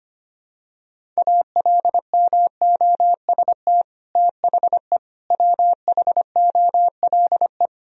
Morse code, heard as a keyed tone, about 25 wpm, 700 Hz. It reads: ALMOST THE WHOLE
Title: Sentences from Top 1000 Words - Review Part 4 - 25wpm